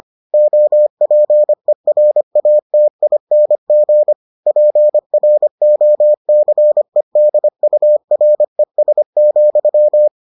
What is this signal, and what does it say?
Morse code, heard as a keyed tone, about 25 wpm, 600 Hz. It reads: OPERATING PROCEDURES,